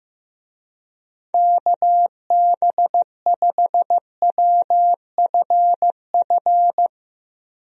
Morse code, heard as a keyed tone, about 15 wpm, 700 Hz. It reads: KB5WFF
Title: USA Call Signs - 6 Characters - Repeated 3x - 15wpm